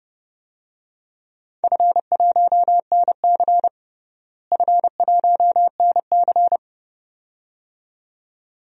Morse code, heard as a keyed tone, about 30 wpm, 700 Hz. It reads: F1NC F1NC
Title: DXCC-100 Call Signs - 4 Characters - Repeated 2x - 30wpm